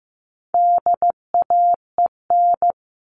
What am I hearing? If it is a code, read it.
Morse code, heard as a keyed tone, about 15 wpm, 700 Hz. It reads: DAEN
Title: All Cut Numbers - 4-Digit 15wpm